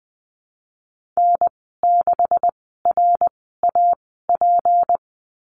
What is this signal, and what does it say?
Morse code, heard as a keyed tone, about 20 wpm, spaced as for 15 wpm, 700 Hz. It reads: N6RAP
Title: 154 Lesson 25g - Letter p - Call Signs 20wpm